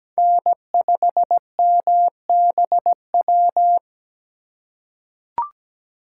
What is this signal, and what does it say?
Morse code, heard as a keyed tone, about 17 wpm, 700 Hz. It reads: N5MBW
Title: USA Call Signs - 5 Characters - 3x Character Spacing - 17wpm